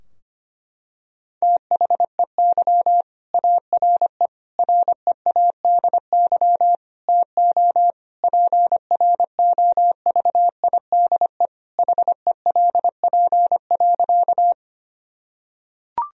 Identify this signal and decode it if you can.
Morse code, heard as a keyed tone, about 25 wpm, 700 Hz. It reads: THEY ARE READY TO PROVIDE HELP.
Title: Sentences from Top 1000 Words - Review Part 3 - 25wpm